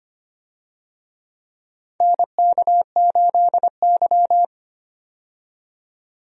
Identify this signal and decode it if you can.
Morse code, heard as a keyed tone, about 25 wpm, 700 Hz. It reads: NK8Y
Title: USA Call Signs - 4 Characters - 25wpm